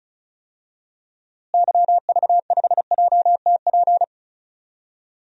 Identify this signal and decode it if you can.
Morse code, heard as a keyed tone, about 34 wpm, 700 Hz. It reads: YV5JTP
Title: DXCC-100 Call Signs - 6 Characters - Repeated 3x - 35wpm